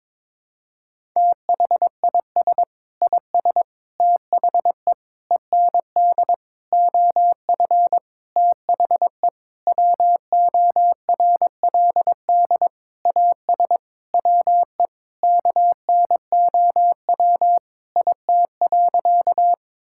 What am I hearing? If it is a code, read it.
Morse code, heard as a keyed tone, about 21 wpm, 700 Hz. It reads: THIS IS THE END OF THE WORLD AS WE KNOW IT.
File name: Sentences with 9-Word Phrases - 22wpm